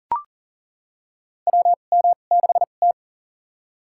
Morse code, heard as a keyed tone, about 40 wpm, spaced as for 29 wpm, 700 Hz. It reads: WM6T
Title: USA Call Signs - 4 Characters - 2x Character Spacing - 40wpm